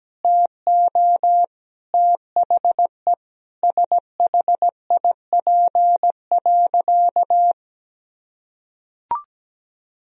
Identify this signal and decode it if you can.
Morse code, heard as a keyed tone, about 17 wpm, 700 Hz. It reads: TO THE SHIP.
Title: Sentences from Top 1000 Words - Review Part 2 - 17wpm